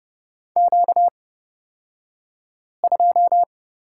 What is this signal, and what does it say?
Morse code, heard as a keyed tone, about 31 wpm, spaced as for 5 wpm, 700 Hz. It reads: Q2
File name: Lesson 38d - Letter q - Two Characters 30wpm